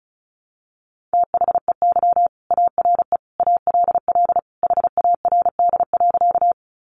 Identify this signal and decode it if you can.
Morse code, heard as a keyed tone, about 34 wpm, 700 Hz. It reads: THEY ARE ALL HARD.
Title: Additional Easy Phrases 35wpm